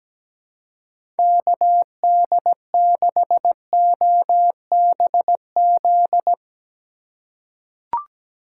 Morse code, heard as a keyed tone, about 17 wpm, 700 Hz. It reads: KD6OBZ
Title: USA Call Signs - 6 Characters - 4x Character Spacing - 17wpm